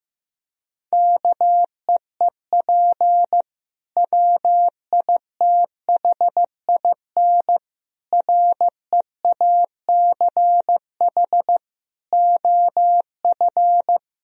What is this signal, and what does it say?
Morse code, heard as a keyed tone, about 15 wpm, 700 Hz. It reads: KEEP WITHIN REACH OF
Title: Sentences from Top 1000 Words - Review Part 4 - 15wpm